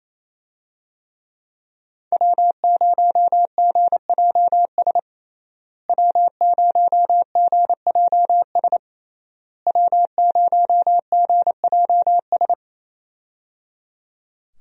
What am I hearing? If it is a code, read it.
Morse code, heard as a keyed tone, about 28 wpm, 700 Hz. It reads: W0GJS W0GJS W0GJS
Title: USA Call Signs - 5 Characters - Repeated 3x - 28wpm